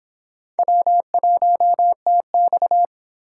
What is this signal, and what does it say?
Morse code, heard as a keyed tone, about 25 wpm, 700 Hz. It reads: W1TX